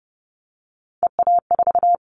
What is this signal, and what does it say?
Morse code, heard as a keyed tone, about 30 wpm, 700 Hz. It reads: EA4
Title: Callsign Prefixes for Top 25 DXCC - 30wpm